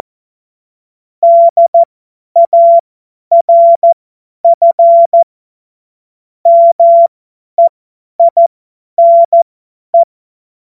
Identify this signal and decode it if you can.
Morse code, heard as a keyed tone, about 14 wpm, spaced as for 10 wpm, 680 Hz. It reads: DARF MEINE
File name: Morsekurs / Witze / Witz 13. September 2022 50 F70